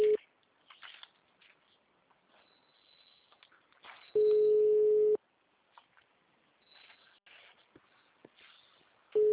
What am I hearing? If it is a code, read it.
Ringback tone (Europe, 425 Hz).
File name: out- -2011- - - .11808